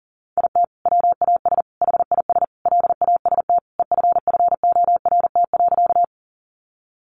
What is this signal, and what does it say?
Morse code, heard as a keyed tone, about 40 wpm, 700 Hz. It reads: IT WAS HIS LAST EFFORT.
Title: Sentences from Top 1200 Words 40wpm